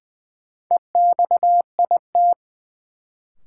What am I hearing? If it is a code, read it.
Morse code, heard as a keyed tone, about 20 wpm, 700 Hz. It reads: EXIT
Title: Sentences from Top 1100 Words - New Word Drill Alt - 20wpm